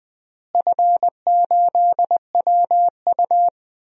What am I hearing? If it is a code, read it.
Morse code, heard as a keyed tone, about 21 wpm, 700 Hz. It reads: F8WU
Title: DXCC-100 Call Signs - 4 Characters - Repeated 2x - 20wpm